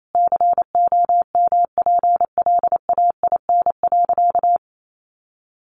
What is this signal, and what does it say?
Morse code, heard as a keyed tone, about 28 wpm, 700 Hz. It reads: COMPLAIN.
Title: Sentences from Top 1300 Words 28wpm